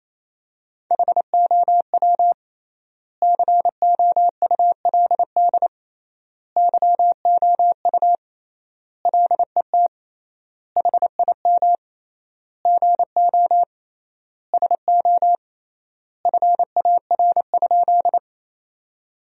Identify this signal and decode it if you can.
Morse code, heard as a keyed tone, about 27 wpm, 700 Hz. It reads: HOW COULD YOU LET HIM GO SO FAR?
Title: 3x Word-Spacing - Sentences from Top 300 Words - 28wpm